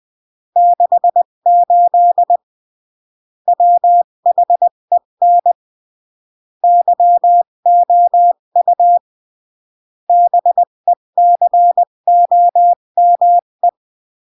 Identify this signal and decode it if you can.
Morse code, heard as a keyed tone, about 20 wpm, 700 Hz. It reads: 68 WHEN YOU BECOME
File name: CW / jodorowsky / 82 commandments 20-15